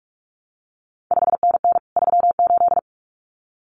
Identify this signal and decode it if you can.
Morse code, heard as a keyed tone, about 46 wpm, 700 Hz. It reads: 5NN 38
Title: CQ WW DX Contest Exchange Encore - 45wpm